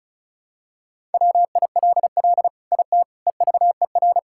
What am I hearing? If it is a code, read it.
Morse code, heard as a keyed tone, about 36 wpm, 690 Hz. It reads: WILL IT EVER
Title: Sentences from Top 1000 Words - Review Part 5 - 35wpm